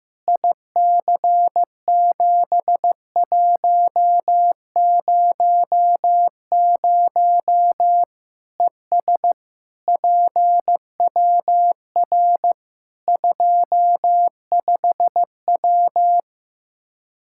Morse code, heard as a keyed tone, about 15 wpm, 700 Hz. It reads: IC7100 ES PWR 25W
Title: K1USN SST QSO Practice 15wpm